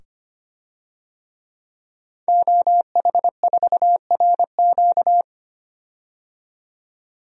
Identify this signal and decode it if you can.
Morse code, heard as a keyed tone, about 25 wpm, 700 Hz. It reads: OH4RQ
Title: DXCC-100 Call Signs - 5 Characters - 25wpm